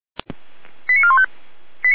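Five-tone selective call (ZVEI), heard as quick runs of five tones, tones of about 70 ms. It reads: 89415
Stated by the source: ZVEI five-tone selective call received on an SDR, logged by the feuerwehr-orlamuende (Orlamünde fire brigade) server.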